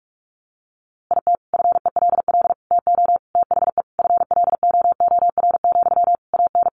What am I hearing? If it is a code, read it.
Morse code, heard as a keyed tone, about 44 wpm, 700 Hz. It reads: IT FELL TO THE FLOOR, AN
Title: A Sound Of Thunder 45wpm